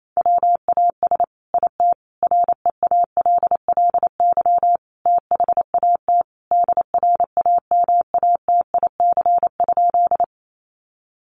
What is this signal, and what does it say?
Morse code, heard as a keyed tone, about 28 wpm, 700 Hz. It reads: WAS IT REALLY THAT DRAMATIC?
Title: Sentences from Top 1300 Words 28wpm